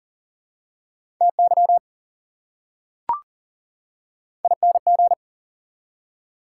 Morse code, heard as a keyed tone, about 37 wpm, 700 Hz. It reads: TY ING
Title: Prefixes and Suffixes 40wpm